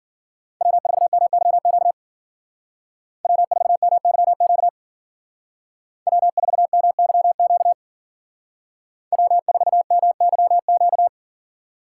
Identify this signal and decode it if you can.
Morse code, heard as a keyed tone, about 52 wpm, 700 Hz. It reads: W4MYQ W4MYQ W4MYQ W4MYQ